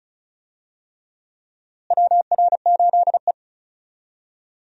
Morse code, heard as a keyed tone, about 36 wpm, 700 Hz. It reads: WR8E
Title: USA Call Signs - 4 Characters - Repeated 3x - 35wpm